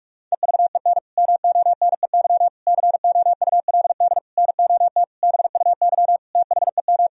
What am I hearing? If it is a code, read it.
Morse code, heard as a keyed tone, about 42 wpm, 700 Hz. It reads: EVEN MONEY COULD NOT BUY THEM
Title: Sentences from Top 1000 Words - Review Part 4 - 45wpm